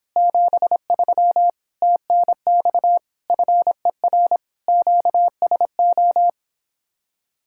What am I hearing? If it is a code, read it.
Morse code, heard as a keyed tone, about 26 wpm, 700 Hz. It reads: 73 TNX FER QSO